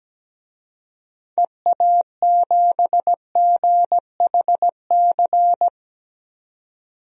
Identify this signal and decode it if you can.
Morse code, heard as a keyed tone, about 17 wpm, 700 Hz. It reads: EA7GHC
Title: DXCC-100 Call Signs - 6 Characters - 17wpm